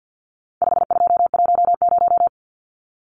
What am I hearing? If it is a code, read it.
Morse code, heard as a keyed tone, about 50 wpm, 700 Hz. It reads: 5210
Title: Numbers - 4-Digit 50wpm